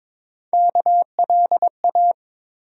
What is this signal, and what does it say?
Morse code, heard as a keyed tone, about 22 wpm, 700 Hz. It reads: KLA
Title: USA Call Sign Postfixes - 3 Letters 22wpm